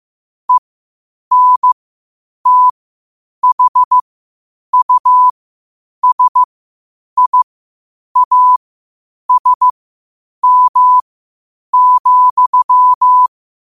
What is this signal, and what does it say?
Morse code, heard as a keyed tone, about 15 wpm, spaced as for 8 wpm, 1000 Hz. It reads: ENTHUSIASM,
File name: qotdfiles / 8wpm